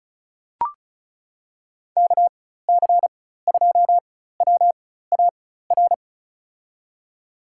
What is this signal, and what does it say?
Morse code, heard as a keyed tone, about 36 wpm, spaced as for 16 wpm, 700 Hz. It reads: KC2WAR
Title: USA Call Signs - 6 Characters - 4x Character Spacing - 35wpm